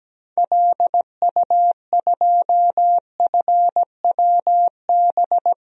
Morse code, heard as a keyed tone, about 17 wpm, 700 Hz. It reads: LU2FWB
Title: DXCC-100 Call Signs - 6 Characters - Repeated 2x - 17wpm